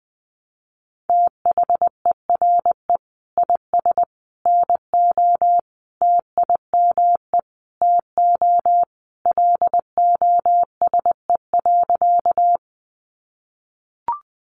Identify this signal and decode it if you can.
Morse code, heard as a keyed tone, about 20 wpm, 700 Hz. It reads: THERE IS NO TIME TO LOSE.
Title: Sentences from Top 1100 Words 20wpm